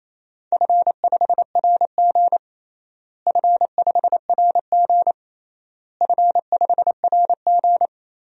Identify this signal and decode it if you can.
Morse code, heard as a keyed tone, about 28 wpm, 700 Hz. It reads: F5RG F5RG F5RG
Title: DXCC-100 Call Signs - 4 Characters - Repeated 3x - 28wpm